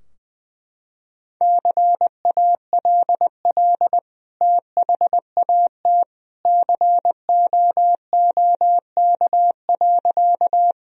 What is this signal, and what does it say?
Morse code, heard as a keyed tone, about 21 wpm, 700 Hz. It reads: CALL THAT COOK.
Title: Sentences from Top 1000 Words - Review Part 4 - 20wpm